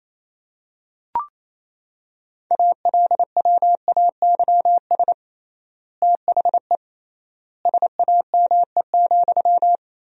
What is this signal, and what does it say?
Morse code, heard as a keyed tone, about 28 wpm, 700 Hz. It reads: ALWAYS THE SAME,
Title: 3x Word-Spacing - Sentences from Top 300 Words - 28wpm